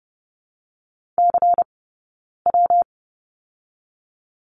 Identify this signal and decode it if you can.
Morse code, heard as a keyed tone, about 30 wpm, 700 Hz. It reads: C W
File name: Lesson 19d - Letter w - Two Characters 30wpm